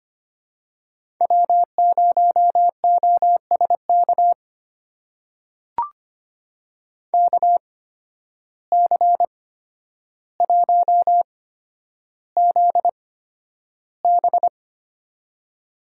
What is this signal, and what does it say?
Morse code, heard as a keyed tone, about 25 wpm, spaced as for 11 wpm, 700 Hz. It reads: W0OSK KC1ZB
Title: USA Call Signs - 5 Characters - 8x Character Spacing - 25wpm